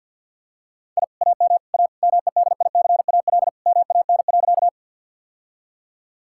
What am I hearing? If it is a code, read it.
Morse code, heard as a keyed tone, about 50 wpm, 700 Hz. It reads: I AM A MEDICAL MAN.
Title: Sentences from Top 1000 Words - Review Part 5 - 50wpm